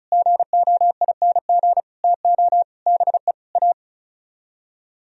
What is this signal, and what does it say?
Morse code, heard as a keyed tone, about 45 wpm, 700 Hz. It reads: GOING TO BE A